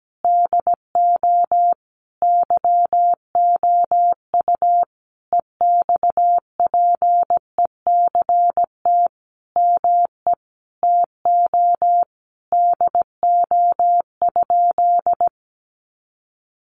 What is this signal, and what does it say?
Morse code, heard as a keyed tone, about 17 wpm, 700 Hz. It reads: DO YOU EXPECT ME TO DO?
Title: Sentences from Top 1100 Words 17wpm